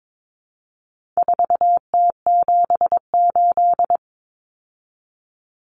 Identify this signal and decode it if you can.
Morse code, heard as a keyed tone, about 22 wpm, 700 Hz. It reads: 4T78
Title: Common Cut Numbers - 4-Digit 22wpm